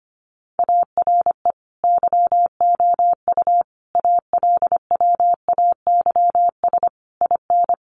Morse code, heard as a keyed tone, about 25 wpm, 700 Hz. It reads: ARE YOU ALWAYS IN
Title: Sentences from Top 1400 Words 25wpm